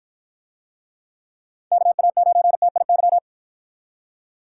Morse code, heard as a keyed tone, about 53 wpm, 700 Hz. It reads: KA9TIY